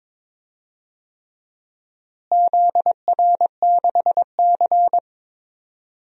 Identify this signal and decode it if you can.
Morse code, heard as a keyed tone, about 21 wpm, 700 Hz. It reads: ZR6C
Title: DXCC-100 Call Signs - 4 Characters - Repeated 3x - 22wpm